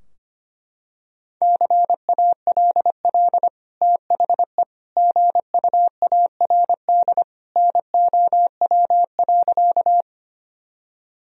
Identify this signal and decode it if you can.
Morse code, heard as a keyed tone, about 25 wpm, 700 Hz. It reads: CALL THE GUARD NOW.
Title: Sentences from Top 1000 Words - Review Part 4 - 25wpm